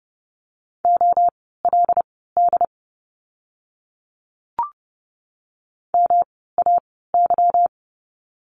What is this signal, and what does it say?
Morse code, heard as a keyed tone, about 30 wpm, spaced as for 17 wpm, 700 Hz. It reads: OLD MAY
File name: course / 30:10wpm / 230 Lesson 37f - Letter x - Words 30wpm